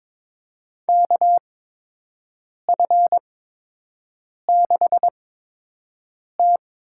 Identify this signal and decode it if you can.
Morse code, heard as a keyed tone, about 21 wpm, spaced as for 6 wpm, 700 Hz. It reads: KF6T
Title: USA Call Signs - 8x Character Spacing - 22wpm